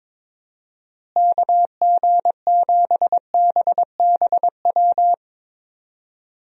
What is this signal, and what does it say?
Morse code, heard as a keyed tone, about 22 wpm, 700 Hz. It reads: KG7BBW
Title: USA Call Signs - 6 Characters - Repeated 3x - 22wpm